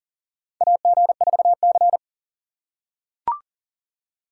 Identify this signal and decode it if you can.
Morse code, heard as a keyed tone, about 39 wpm, 700 Hz. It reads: AG4C
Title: USA Call Signs - 4 Characters - 4x Character Spacing - 40wpm